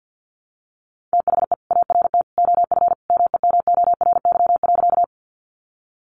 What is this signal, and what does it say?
Morse code, heard as a keyed tone, about 50 wpm, 700 Hz. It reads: THE ART OF MEMORY.
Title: Sentences from Top 1300 Words 50wpm